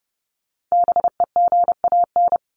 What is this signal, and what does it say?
Morse code, heard as a keyed tone, about 30 wpm, 700 Hz. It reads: BEGAN
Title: Top 300 Words 30wpm